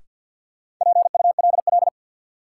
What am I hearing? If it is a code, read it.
Morse code, heard as a keyed tone, about 53 wpm, 700 Hz. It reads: PULL
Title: Sentences from Top 400 Words - New Word Drill - 50wpm